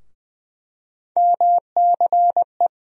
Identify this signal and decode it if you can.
Morse code, heard as a keyed tone, about 21 wpm, 700 Hz. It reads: MCE
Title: USA Call Sign Postfixes - 3 Letters 20wpm